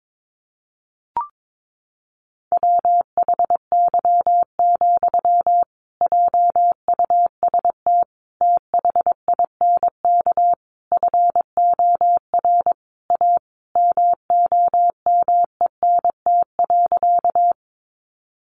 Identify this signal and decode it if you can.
Morse code, heard as a keyed tone, about 22 wpm, 700 Hz. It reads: WHY, JUST THINK FOR A MOMENT.
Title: Sentences from Top 1000 Words 22wpm